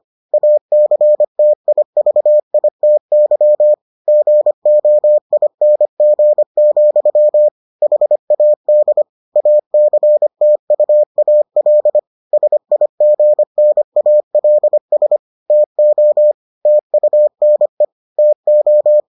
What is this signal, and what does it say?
Morse code, heard as a keyed tone, about 25 wpm, 600 Hz. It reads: ACTIVITY GOING, HAD ACTUAL SIGNALS TO TUNE TO